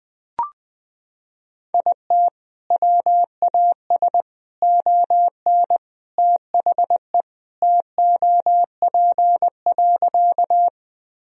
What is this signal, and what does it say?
Morse code, heard as a keyed tone, about 20 wpm, 700 Hz. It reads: IT WAS ON THE TOP.
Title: Sentences from Top 1000 Words - Review Part 2 - 20wpm